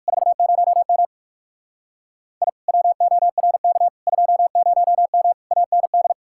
Morse code, heard as a keyed tone, about 57 wpm, 700 Hz. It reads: 40M I WORK 20M AND